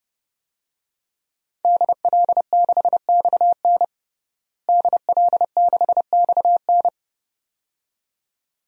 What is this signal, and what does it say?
Morse code, heard as a keyed tone, about 32 wpm, 700 Hz. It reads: DL6XN DL6XN
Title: DXCC-100 Call Signs - 5 Characters - Repeated 2x - 30wpm